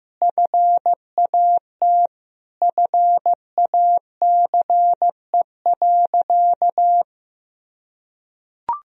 Morse code, heard as a keyed tone, about 15 wpm, 700 Hz. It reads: FAT FACE.
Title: Sentences from Top 1000 Words - Review Part 4 - 15wpm